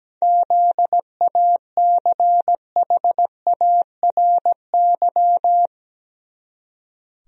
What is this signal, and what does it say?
Morse code, heard as a keyed tone, about 17 wpm, 700 Hz. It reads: ZACHARY
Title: USA Common Names - 17wpm